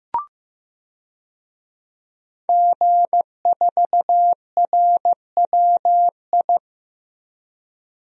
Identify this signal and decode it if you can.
Morse code, heard as a keyed tone, about 15 wpm, 700 Hz. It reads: G4RWI
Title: DXCC-100 Call Signs - 5 Characters - Repeated 2x - 15wpm